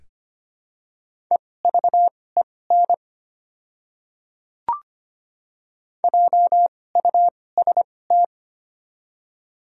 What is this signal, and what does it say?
Morse code, heard as a keyed tone, about 25 wpm, spaced as for 18 wpm, 700 Hz. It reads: EVEN JUST
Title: 2x Character Spacing - Top 200 Words - 25wpm